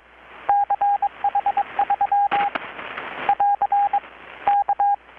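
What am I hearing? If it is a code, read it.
Morse code, heard as a keyed tone, about 22 wpm, 810 Hz, with radio noise. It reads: CH3 <AR> K
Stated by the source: Morse from KPH's Night of Nights XX maritime radiotelegraphy event, received in Chiapas, Mexico about 3,800 km away on a Kenwood R-600 with a grounded 10 m long-wire antenna.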